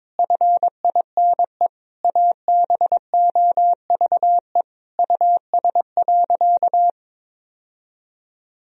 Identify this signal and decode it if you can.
Morse code, heard as a keyed tone, about 22 wpm, 700 Hz. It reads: FINE ABOVE US.
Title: Sentences from Top 1000 Words - Review Part 2 - 22wpm